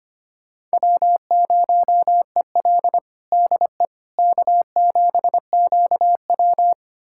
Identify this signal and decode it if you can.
Morse code, heard as a keyed tone, about 25 wpm, 700 Hz. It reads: W0EL DE K7QW
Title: USA Callsign DE Callsign - 4 Characters - 25wpm